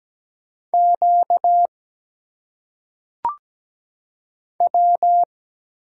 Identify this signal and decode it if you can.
Morse code, heard as a keyed tone, about 17 wpm, 700 Hz. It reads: Q W
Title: Single Letters - Phonetic Pronunciation 17wpm